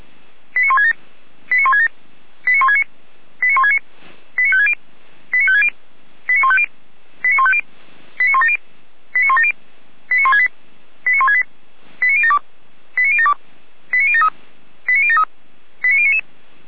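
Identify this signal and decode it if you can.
Five-tone selective call (ZVEI), heard as quick runs of five tones, tones of about 70 ms. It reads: 89167 89167 89169 89169 89560 89560 89150 89150 89170 89170 89168 89168 89072 89072 89073 89073 89099
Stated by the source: Five-tone ZVEI selective call received on an SDR, fire brigade alerting.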